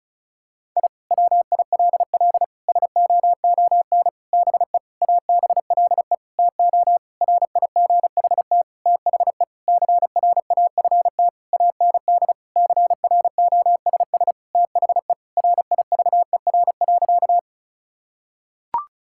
Morse code, heard as a keyed tone, about 36 wpm, 700 Hz. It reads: I WILL SOON BE ABLE TO RIGHT THE CRAFT AND CROSS THE RIVER.
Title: Sentences from Top 1000 Words - Review Part 6 - 35wpm